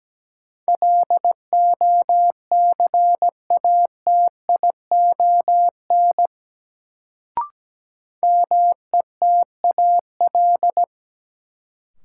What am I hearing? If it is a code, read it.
Morse code, heard as a keyed tone, about 17 wpm, 700 Hz. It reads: LOCATION METAL
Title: Sentences from Top 1000 Words - Review Part 6 - New Word Drill - 17wpm